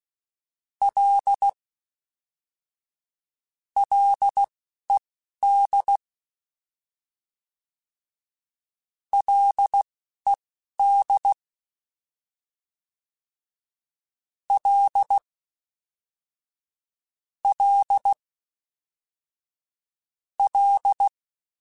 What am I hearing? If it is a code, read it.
Morse code, heard as a keyed tone, about 16 wpm, spaced as for 11 wpm, 800 Hz. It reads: L LED LED L L L